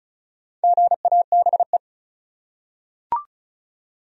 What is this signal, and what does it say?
Morse code, heard as a keyed tone, about 36 wpm, 700 Hz. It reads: GABE
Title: USA Common Names - 35wpm